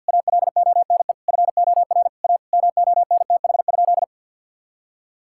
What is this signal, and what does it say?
Morse code, heard as a keyed tone, about 50 wpm, 700 Hz. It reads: ALONE FOR A MONTH?
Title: Sentences from Top 1000 Words - Review Part 5 - 50wpm